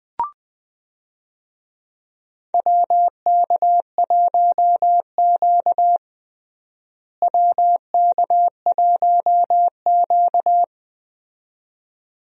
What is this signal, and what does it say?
Morse code, heard as a keyed tone, about 20 wpm, 700 Hz. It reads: WK1Q WK1Q
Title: USA Call Signs - 4 Characters - Repeated 2x - 20wpm